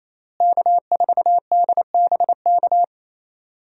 Morse code, heard as a keyed tone, about 28 wpm, 700 Hz. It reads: K4DBK
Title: USA Call Signs - 5 Characters - Repeated 2x - 28wpm